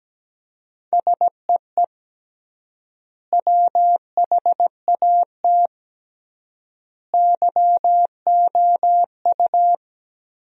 Morse code, heard as a keyed tone, about 17 wpm, 700 Hz. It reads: SEE WHAT YOU
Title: 3x Word-Spacing - Sentences from Top 500 Words - 17wpm